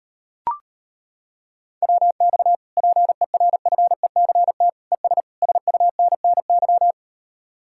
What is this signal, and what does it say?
Morse code, heard as a keyed tone, about 33 wpm, 700 Hz. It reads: WX PERFECT ES SUNNY